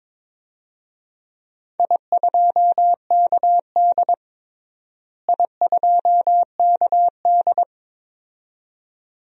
Morse code, heard as a keyed tone, about 22 wpm, 700 Hz. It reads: I2KD I2KD
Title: DXCC-100 Call Signs - 4 Characters - Repeated 2x - 22wpm